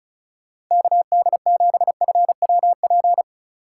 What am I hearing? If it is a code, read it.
Morse code, heard as a keyed tone, about 45 wpm, 700 Hz. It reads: KD7FWP